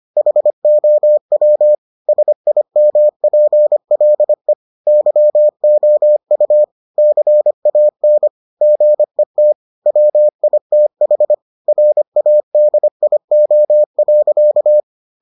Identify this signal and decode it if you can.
Morse code, heard as a keyed tone, about 25 wpm, 600 Hz. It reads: HOW SIMPLE YOU CAN GET WITH RADIO.